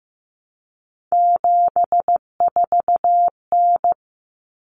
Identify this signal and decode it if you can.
Morse code, heard as a keyed tone, about 15 wpm, 700 Hz. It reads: 74N
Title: Common Cut Numbers - 3-Digit 15wpm